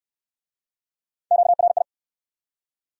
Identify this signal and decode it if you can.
Morse code, heard as a keyed tone, about 56 wpm, 700 Hz. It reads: QRI
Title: Speed-Racing Q-Codes - 45wpm